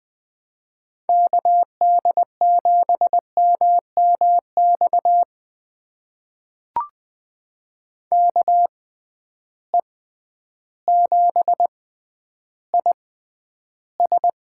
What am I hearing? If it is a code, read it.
Morse code, heard as a keyed tone, about 20 wpm, spaced as for 11 wpm, 700 Hz. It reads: KD7MMX KE7IS
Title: USA Call Signs - 6 Characters - 6x Character Spacing - 20wpm